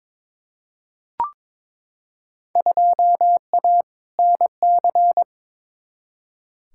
Morse code, heard as a keyed tone, about 22 wpm, 700 Hz. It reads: 2A NC
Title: Field Day Exchanges - 22wpm